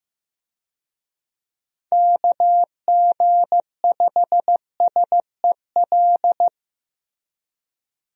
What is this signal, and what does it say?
Morse code, heard as a keyed tone, about 16 wpm, 700 Hz. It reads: KG5SEL
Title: USA Call Signs - 6 Characters - Repeated 3x - 15wpm